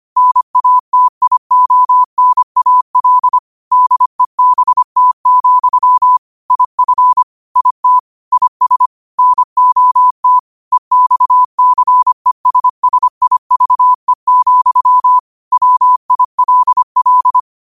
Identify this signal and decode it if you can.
Morse code, heard as a keyed tone, about 25 wpm, 1000 Hz. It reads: NATIONAL DEBT, IF IT IS NOT EXCESSIVE, WILL